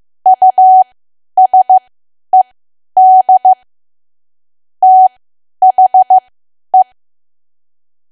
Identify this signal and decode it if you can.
Morse code, heard as a keyed tone, about 15 wpm, spaced as for 10 wpm, 750 Hz. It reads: USED THE
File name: Archive / 10 WPM / 10WPM